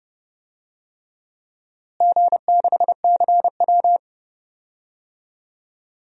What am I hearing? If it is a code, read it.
Morse code, heard as a keyed tone, about 30 wpm, 700 Hz. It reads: G6CW
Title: DXCC-100 Call Signs - 4 Characters - Repeated 3x - 30wpm